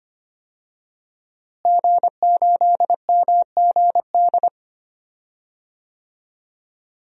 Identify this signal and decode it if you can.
Morse code, heard as a keyed tone, about 25 wpm, 700 Hz. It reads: G8MGD
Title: DXCC-100 Call Signs - 5 Characters - Repeated 2x - 25wpm